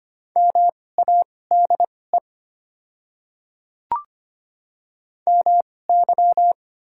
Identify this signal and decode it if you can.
Morse code, heard as a keyed tone, about 25 wpm, spaced as for 18 wpm, 690 Hz. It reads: MADE MY
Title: 2x Character Spacing - Top 200 Words - 25wpm